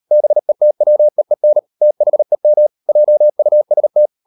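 Morse code, heard as a keyed tone, about 37 wpm, 600 Hz. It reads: BETWEEN THEM JUST
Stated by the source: Morse code practice at 38 WPM, sending Ken Follett's novel Never in English, from CW med Gnister.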